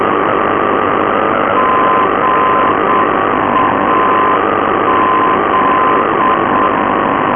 Morse code, heard as a keyed tone, about 7 wpm, 1030 Hz, with noise. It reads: GAZ
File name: NDB Gazoldo GAZ